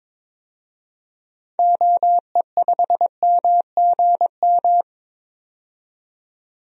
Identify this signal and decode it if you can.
Morse code, heard as a keyed tone, about 21 wpm, 700 Hz. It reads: OE5MGM
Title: DXCC-100 Call Signs - 6 Characters - 22wpm